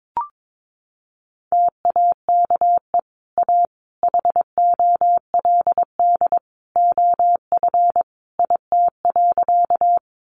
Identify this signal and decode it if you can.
Morse code, heard as a keyed tone, about 22 wpm, 700 Hz. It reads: TAKE A HOLD OF IT.
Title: Sentences from Top 400 Words 22wpm